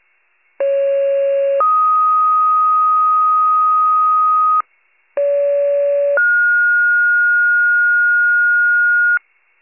Two-tone page, heard as a one-second tone, then a three-second tone, about 550 then 1250 Hz, about 570 then 1430 Hz, with noise. It is heard from 0.6 to 9.2 s.